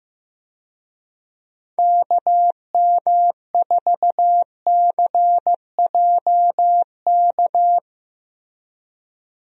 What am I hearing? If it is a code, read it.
Morse code, heard as a keyed tone, about 15 wpm, 700 Hz. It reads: KM4CJK
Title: USA Call Signs - 6 Characters - Repeated 3x - 15wpm